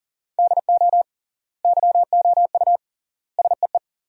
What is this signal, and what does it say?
Morse code, heard as a keyed tone, about 40 wpm, 700 Hz. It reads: DO YOU SEE
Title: 3x Word-Spacing - Sentences from Top 100 Words - 40wpm